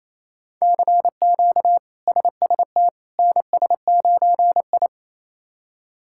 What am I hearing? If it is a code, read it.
Morse code, heard as a keyed tone, about 28 wpm, 700 Hz. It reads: CQ SST NS9I
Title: K1USN SST QSO Practice - 28wpm